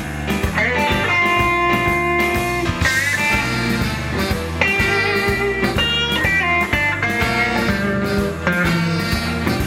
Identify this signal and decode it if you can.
Dial-up modem handshake.